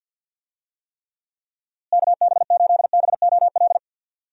Morse code, heard as a keyed tone, about 53 wpm, 700 Hz. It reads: KB8BOL
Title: USA Call Signs - 6 Characters - Repeated 2x - 50wpm